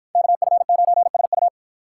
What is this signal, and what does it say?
Morse code, heard as a keyed tone, about 53 wpm, 700 Hz. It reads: KF9SU